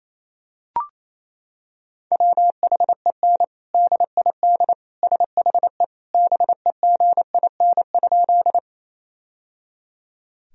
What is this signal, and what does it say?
Morse code, heard as a keyed tone, about 28 wpm, 700 Hz. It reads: WHEN DID SHE BEGIN?
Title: Sentences from Top 1000 Words - Review Part 1 - 28wpm